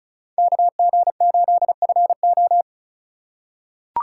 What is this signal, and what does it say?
Morse code, heard as a keyed tone, about 36 wpm, 700 Hz. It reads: KG8FO
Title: USA Call Signs - 5 Characters - 2x Character Spacing - 35wpm